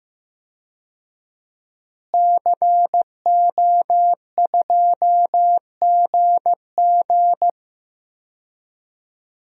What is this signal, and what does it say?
Morse code, heard as a keyed tone, about 15 wpm, 700 Hz. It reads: CO2GG
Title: DXCC-100 Call Signs - 5 Characters - Repeated 3x - 15wpm